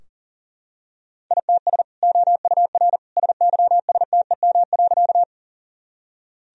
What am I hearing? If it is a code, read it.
Morse code, heard as a keyed tone, about 40 wpm, 700 Hz. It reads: ITS OUR SYSTEM.
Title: Sentences from Top 1000 Words - Review Part 3 - 40wpm